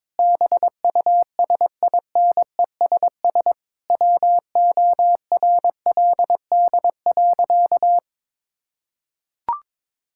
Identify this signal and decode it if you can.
Morse code, heard as a keyed tone, about 22 wpm, 700 Hz. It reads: BUSINESS WORLD.
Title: Sentences from Top 1000 Words - Review Part 6 - 22wpm